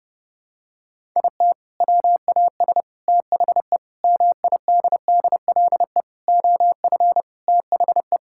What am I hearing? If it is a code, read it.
Morse code, heard as a keyed tone, about 30 wpm, 700 Hz. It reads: IT WAS THE MIDDLE OF THE
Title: Sentences from Top 1000 Words - Review Part 6 - 30wpm